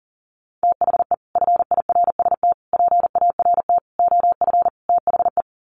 Morse code, heard as a keyed tone, about 40 wpm, 700 Hz. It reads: THE FIRST PART OF THE
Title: Sets of 5 Words Encore 40wpm